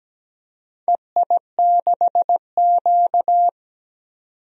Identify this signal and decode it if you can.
Morse code, heard as a keyed tone, about 17 wpm, 700 Hz. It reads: EI6Q
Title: DXCC-100 Call Signs - 4 Characters - 17wpm